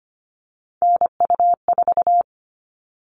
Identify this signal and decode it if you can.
Morse code, heard as a keyed tone, about 25 wpm, 700 Hz. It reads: NU4
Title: All Cut Numbers - 3-Digit 25wpm